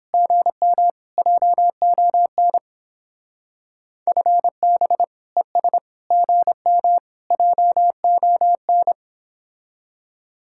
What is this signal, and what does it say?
Morse code, heard as a keyed tone, about 29 wpm, 700 Hz. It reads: GM JON FB ES GM JON